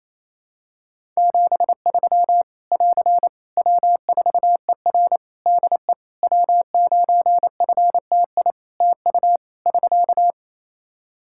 Morse code, heard as a keyed tone, about 28 wpm, 700 Hz. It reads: 73 <AR> W4ER DE W9FTI TU <SK>
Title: Practice QSO - Exchange 4 - Ending - 28wpm